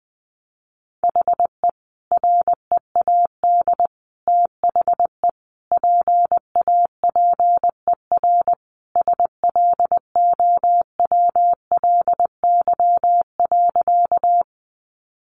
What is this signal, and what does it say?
Morse code, heard as a keyed tone, about 20 wpm, 700 Hz. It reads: HE READ THE PAPER SLOWLY.
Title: Sentences from Top 1100 Words 20wpm